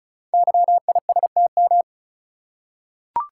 Morse code, heard as a keyed tone, about 36 wpm, 700 Hz. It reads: YISTM
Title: Code Groups - 5 Letters - 35wpm